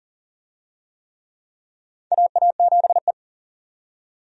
Morse code, heard as a keyed tone, about 40 wpm, 700 Hz. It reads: AA7E
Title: USA Call Signs - 4 Characters - 40wpm